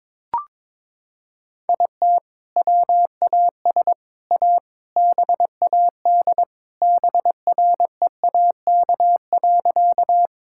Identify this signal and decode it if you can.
Morse code, heard as a keyed tone, about 22 wpm, 700 Hz. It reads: IT WAS A BAD BREAK.
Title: Sentences from Top 1000 Words - Review Part 6 - 22wpm